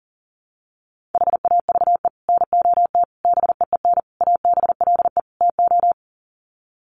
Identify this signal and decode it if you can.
Morse code, heard as a keyed tone, about 40 wpm, 700 Hz. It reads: HAVE NOT BEEN ABLE TO
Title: Sets of 5 Words Encore 40wpm